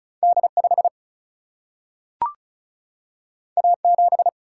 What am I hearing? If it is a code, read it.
Morse code, heard as a keyed tone, about 36 wpm, 700 Hz. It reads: D 5 A 7
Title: Single-Letter Single-Number 35wpm